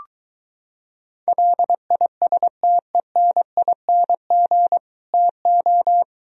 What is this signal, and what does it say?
Morse code, heard as a keyed tone, about 23 wpm, 700 Hz. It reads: LISTENING TO